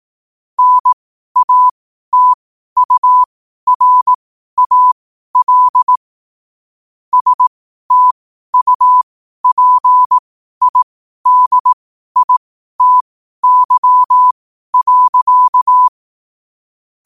Morse code, heard as a keyed tone, about 18 wpm, spaced as for 12 wpm, 1000 Hz. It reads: NATURAL STUPIDITY.
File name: qotdfiles / 12wpm